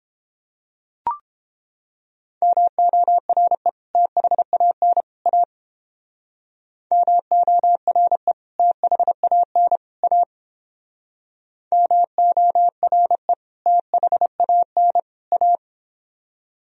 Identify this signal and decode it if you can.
Morse code, heard as a keyed tone, about 31 wpm, 700 Hz. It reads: MORE THAN A MORE THAN A MORE THAN A